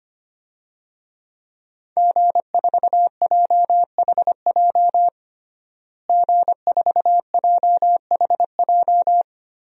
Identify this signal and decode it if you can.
Morse code, heard as a keyed tone, about 25 wpm, 700 Hz. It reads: G4JHJ G4JHJ
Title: DXCC-100 Call Signs - 5 Characters - Repeated 3x - 25wpm